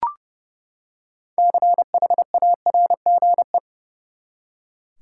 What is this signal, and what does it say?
Morse code, heard as a keyed tone, about 30 wpm, 700 Hz. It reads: CHARGE
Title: Six Letter Words - 30wpm